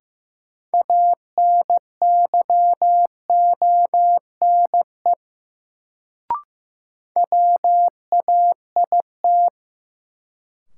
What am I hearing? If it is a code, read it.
Morse code, heard as a keyed tone, about 15 wpm, 700 Hz. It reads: ANYONE WAIT
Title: Sets of 5 Words Encore - New Word Drill - 15wpm